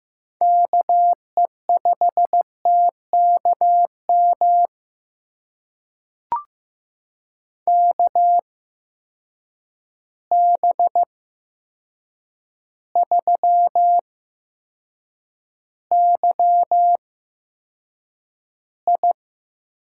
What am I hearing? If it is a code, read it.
Morse code, heard as a keyed tone, about 15 wpm, spaced as for 7 wpm, 700 Hz. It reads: KE5TKM KB3YI
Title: USA Call Signs - 6 Characters - 8x Character Spacing - 15wpm